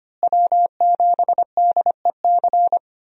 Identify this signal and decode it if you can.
Morse code, heard as a keyed tone, about 25 wpm, 700 Hz. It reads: W7DEC
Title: USA Call Signs - 5 Characters - Repeated 2x - 25wpm